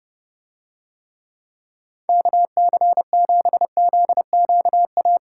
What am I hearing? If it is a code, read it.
Morse code, heard as a keyed tone, about 30 wpm, 700 Hz. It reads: KC7ZQA